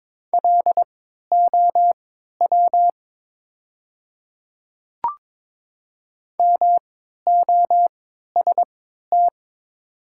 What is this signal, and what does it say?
Morse code, heard as a keyed tone, about 22 wpm, spaced as for 13 wpm, 700 Hz. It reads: LOW MOST
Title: 3x Character Spacing - Top 200 Words - 22wpm